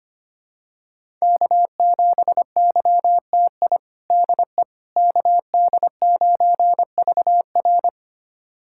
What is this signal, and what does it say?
Morse code, heard as a keyed tone, about 25 wpm, 700 Hz. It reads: K7YTI DE KD9VR
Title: USA Callsign DE Callsign - 5 Characters - 25wpm